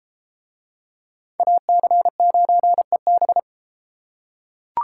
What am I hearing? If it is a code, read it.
Morse code, heard as a keyed tone, about 32 wpm, 700 Hz. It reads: AC9EB